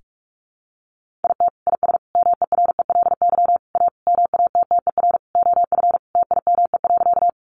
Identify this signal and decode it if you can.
Morse code, heard as a keyed tone, about 44 wpm, 700 Hz. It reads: IT IS MERELY A MATTER OF TIME.
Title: Sentences from Top 1400 Words 45wpm